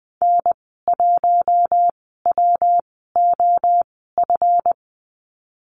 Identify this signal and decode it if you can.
Morse code, heard as a keyed tone, about 20 wpm, spaced as for 14 wpm, 700 Hz. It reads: N1WOF
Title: 193 Lesson 31g - Letter v - Call Signs 20wpm